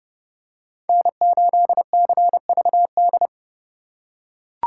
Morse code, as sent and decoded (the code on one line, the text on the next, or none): -. ---.. -.-. ...- -..
N8CVD